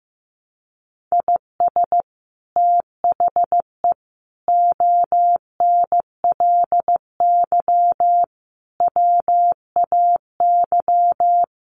Morse code, heard as a keyed tone, about 15 wpm, 700 Hz. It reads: IS THE ONLY WAY
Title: Sets of 5 Words Encore 15wpm